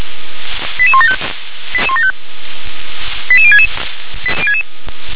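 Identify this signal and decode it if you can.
Five-tone selective call (ZVEI), heard as quick runs of five tones, tones of about 70 ms. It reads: 88175 88066
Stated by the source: Five-tone ZVEI selective call 88625, received on SDR.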